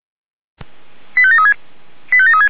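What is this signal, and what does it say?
Five-tone selective call (ZVEI), heard as quick runs of five tones, tones of about 70 ms. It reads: 85736 85736
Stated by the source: ZVEI five-tone selective call 85381, received by SDR.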